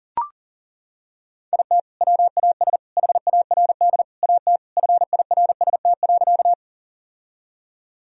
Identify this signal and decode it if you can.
Morse code, heard as a keyed tone, about 40 wpm, 700 Hz. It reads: IT WAS HARD AT FIRST.
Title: Sentences from Top 1000 Words - Review Part 1 - 40wpm